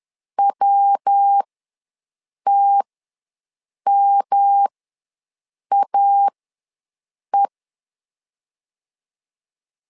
Morse code, heard as a keyed tone, about 10 wpm, spaced as for 6 wpm, 800 Hz. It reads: WTMAE
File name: LEZIONI / 3 MWRO / 3 MW ws 10wpm 2213